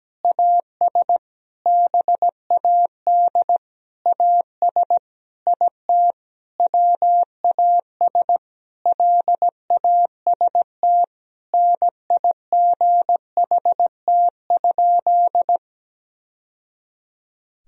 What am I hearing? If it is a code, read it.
Morse code, heard as a keyed tone, about 17 wpm, 700 Hz. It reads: AS BAD AS IT WAS LAST NIGHT?
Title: Sentences from Top 1000 Words - Review Part 1 - 17wpm